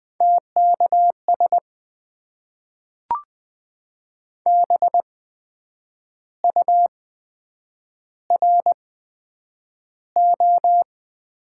Morse code, heard as a keyed tone, about 20 wpm, spaced as for 8 wpm, 700 Hz. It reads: TKS BURO
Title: Top 90 QSO Elements - 8x Character Spacing - 20wpm